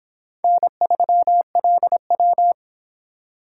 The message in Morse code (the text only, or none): N3LW